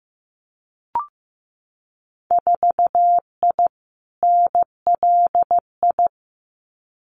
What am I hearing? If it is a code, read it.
Morse code, heard as a keyed tone, about 15 wpm, 700 Hz. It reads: 4I NLI
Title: Winter Field Day Exchanges Encore 15wpm